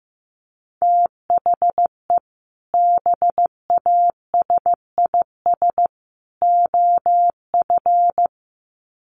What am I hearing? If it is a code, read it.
Morse code, heard as a keyed tone, about 15 wpm, 700 Hz. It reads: THE BASIS OF
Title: Sets of 3 Words 15wpm